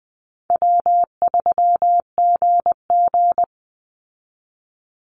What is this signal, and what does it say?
Morse code, heard as a keyed tone, about 20 wpm, 700 Hz. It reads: W3GG
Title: Easy Call Signs Repeated 3x - 20wpm